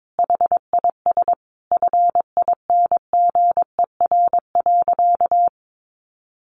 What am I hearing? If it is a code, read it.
Morse code, heard as a keyed tone, about 22 wpm, 700 Hz. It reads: HIS FINGER.
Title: Additional Easy Phrases 22wpm